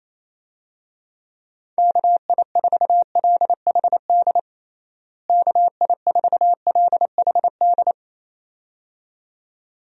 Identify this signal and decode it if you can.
Morse code, heard as a keyed tone, about 28 wpm, 700 Hz. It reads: KI4LHD KI4LHD
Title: USA Call Signs - 6 Characters - Repeated 2x - 28wpm